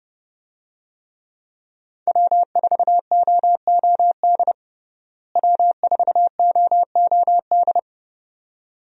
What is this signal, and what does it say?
Morse code, heard as a keyed tone, about 30 wpm, 700 Hz. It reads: W4OOD W4OOD
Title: USA Call Signs - 5 Characters - Repeated 2x - 30wpm